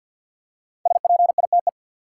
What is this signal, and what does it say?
Morse code, heard as a keyed tone, about 47 wpm, 700 Hz. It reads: SPITE
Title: Five Letter Words - 50wpm